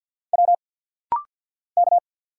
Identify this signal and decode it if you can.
Morse code, heard as a keyed tone, about 50 wpm, 700 Hz. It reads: W K
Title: Single Letters - Rapid-Fire 50wpm